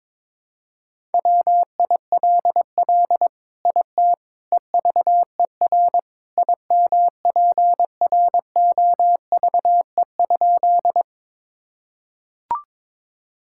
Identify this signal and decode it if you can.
Morse code, heard as a keyed tone, about 22 wpm, 700 Hz. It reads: WILL IT EVER IMPROVE?
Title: Sentences from Top 1000 Words - Review Part 5 - 22wpm